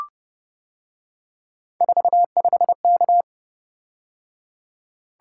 Morse code, heard as a keyed tone, about 29 wpm, 700 Hz. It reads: 45K